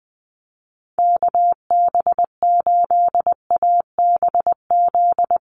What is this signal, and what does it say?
Morse code, heard as a keyed tone, about 20 wpm, 700 Hz. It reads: KB8ABZ
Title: USA Call Signs 20wpm